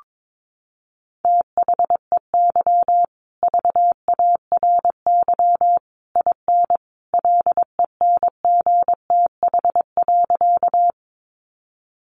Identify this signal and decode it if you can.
Morse code, heard as a keyed tone, about 22 wpm, 700 Hz. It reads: THEY VARY IN LENGTH.
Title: Sentences from Top 1200 Words 22wpm